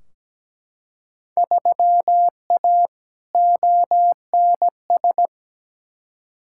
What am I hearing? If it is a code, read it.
Morse code, heard as a keyed tone, about 17 wpm, 700 Hz. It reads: 3A ONS
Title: Field Day Exchanges - 17wpm